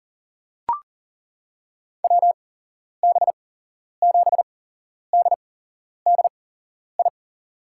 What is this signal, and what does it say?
Morse code, heard as a keyed tone, about 40 wpm, spaced as for 11 wpm, 700 Hz. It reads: WB7DDI
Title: USA Call Signs - 6 Characters - 8x Character Spacing - 40wpm